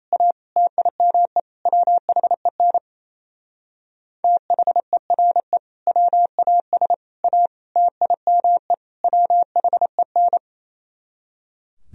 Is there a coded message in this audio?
Yes: Morse code from 0.1 to 10.4 s, heard as a keyed tone, about 35 wpm, 700 Hz. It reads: A TIME WHEN THERE WAS A TIME WHEN